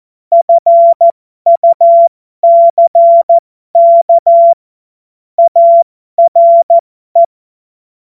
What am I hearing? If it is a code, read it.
Morse code, heard as a keyed tone, about 14 wpm, spaced as for 12 wpm, 680 Hz. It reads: FUCK ARE